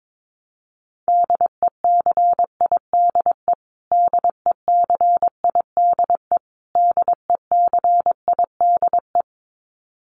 Morse code, heard as a keyed tone, about 22 wpm, 700 Hz. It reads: DECIDE DECIDE DECIDE
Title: Top 500 Words Repeated 22wpm